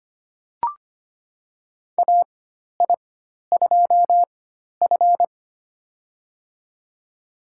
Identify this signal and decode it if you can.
Morse code, heard as a keyed tone, about 25 wpm, spaced as for 12 wpm, 700 Hz. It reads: AI2F
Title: USA Call Signs - 4 Characters - 4x Character Spacing - 25wpm